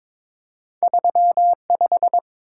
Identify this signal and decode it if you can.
Morse code, heard as a keyed tone, about 22 wpm, 700 Hz. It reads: 35
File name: practice / 22wpm / Two Letter-Number Combinations - 22wpm